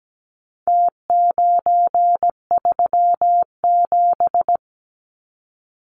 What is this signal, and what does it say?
Morse code, heard as a keyed tone, about 17 wpm, 700 Hz. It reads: T937
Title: Numbers - Contest Style - 4-Digit 17wpm